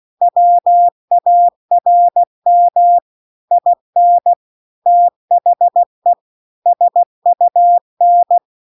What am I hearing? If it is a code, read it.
Morse code, heard as a keyed tone, about 16 wpm, 700 Hz. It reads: WARM IN THE SUN